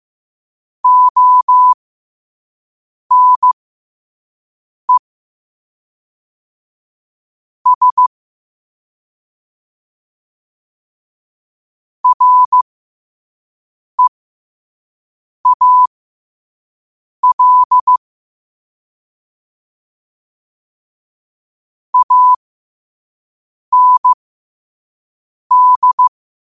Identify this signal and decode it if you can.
Morse code, heard as a keyed tone, about 15 wpm, spaced as for 5 wpm, 1000 Hz. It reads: ONE S REAL AND
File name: qotdfiles / 5wpm